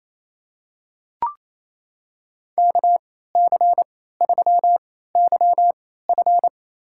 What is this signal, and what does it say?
Morse code, heard as a keyed tone, about 28 wpm, spaced as for 16 wpm, 700 Hz. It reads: KC3YF
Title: USA Call Signs - 3x Character Spacing - 28wpm